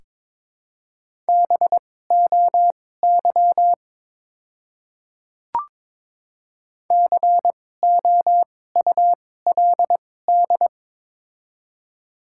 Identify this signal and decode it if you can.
Morse code, heard as a keyed tone, about 22 wpm, spaced as for 16 wpm, 700 Hz. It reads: BOY COULD
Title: 2x Character Spacing - Top 200 Words - 22wpm